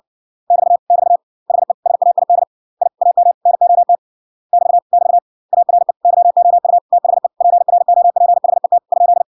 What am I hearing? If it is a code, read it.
Morse code, heard as a keyed tone, about 59 wpm, 700 Hz. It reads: <BT> <BT> HE SAID I AM NOT <BT> <BT> ARE YOU THE PROPHET ?